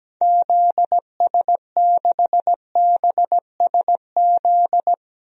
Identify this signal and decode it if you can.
Morse code, heard as a keyed tone, about 17 wpm, 700 Hz. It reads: ZS6BSZ
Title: DXCC-100 Call Signs - 6 Characters - 17wpm